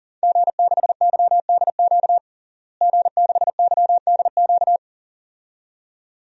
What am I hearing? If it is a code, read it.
Morse code, heard as a keyed tone, about 40 wpm, 700 Hz. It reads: G6YDQ G6YDQ
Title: DXCC-100 Call Signs - 5 Characters - Repeated 2x - 40wpm